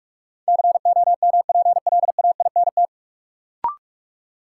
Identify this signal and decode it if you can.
Morse code, heard as a keyed tone, about 46 wpm, 700 Hz. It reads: COMPLAINT
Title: Sentences from Top 1400 Words - New Word Drill Alt - 45wpm